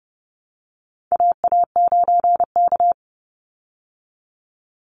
Morse code, heard as a keyed tone, about 30 wpm, 700 Hz. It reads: AA9K
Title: Call Signs Repeated 3x - 30wpm